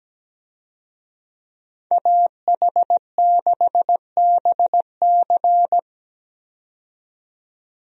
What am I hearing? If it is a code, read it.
Morse code, heard as a keyed tone, about 17 wpm, 700 Hz. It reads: AH6BC
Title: USA Call Signs - 5 Characters - 17wpm